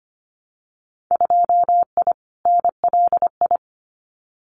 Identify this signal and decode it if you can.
Morse code, heard as a keyed tone, about 25 wpm, 700 Hz. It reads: 2I NLI
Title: Winter Field Day Exchanges 25wpm